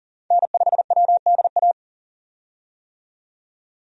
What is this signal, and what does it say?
Morse code, heard as a keyed tone, about 37 wpm, 700 Hz. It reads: N5WDA
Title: USA Call Signs - 5 Characters - 40wpm